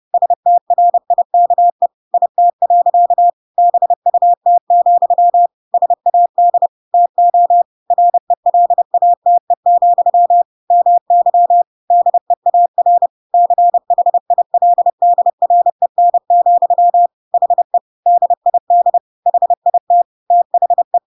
Morse code, heard as a keyed tone, about 30 wpm, 700 Hz. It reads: STRIKE IT. BUT, SAD TO RELATE, MY DEAR CHILDREN, HE DID HIT THE
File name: Pinocchio / Pinocchio ch4 30 30 ik1hge